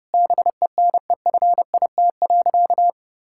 Morse code, heard as a keyed tone, about 30 wpm, 700 Hz. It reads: BENEFIT.
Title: Sentences from Top 1000 Words - Review Part 6 - 30wpm